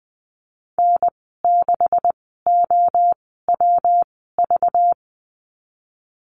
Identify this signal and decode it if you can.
Morse code, heard as a keyed tone, about 20 wpm, spaced as for 14 wpm, 700 Hz. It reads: N6OWV
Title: 207 Lesson 33g - Letter j - Call Signs 20wpm